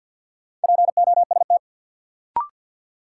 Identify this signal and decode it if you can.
Morse code, heard as a keyed tone, about 52 wpm, 700 Hz. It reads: POST
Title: Sentences from Top 1000 Words - New Word Drill - 50wpm